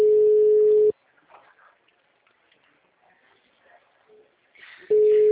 Ringback tone (Europe, 425 Hz).